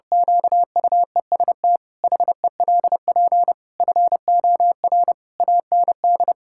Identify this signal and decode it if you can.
Morse code, heard as a keyed tone, about 30 wpm, 700 Hz. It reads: QUEST HELP FOR AND
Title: Pangram Challenge v2 - 30wpm